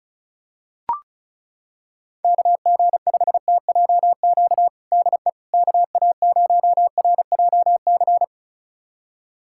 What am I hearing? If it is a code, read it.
Morse code, heard as a keyed tone, about 34 wpm, 700 Hz. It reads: KG5TJQ DE KA0RJC
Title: USA Callsign DE Callsign - 6 Characters - 35wpm